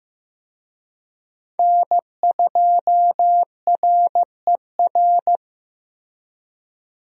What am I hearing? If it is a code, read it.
Morse code, heard as a keyed tone, about 15 wpm, 700 Hz. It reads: N2RER
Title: USA Call Signs - 5 Characters - Repeated 2x - 15wpm